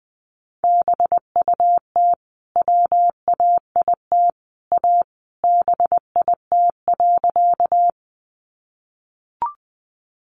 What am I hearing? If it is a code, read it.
Morse code, heard as a keyed tone, about 20 wpm, 700 Hz. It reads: BUT WAIT A BIT.
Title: Sentences from Top 900 Words 20wpm